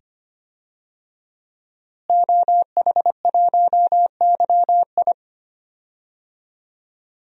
Morse code, heard as a keyed tone, about 25 wpm, 700 Hz. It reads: OH1YI
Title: DXCC-100 Call Signs - 5 Characters - Repeated 2x - 25wpm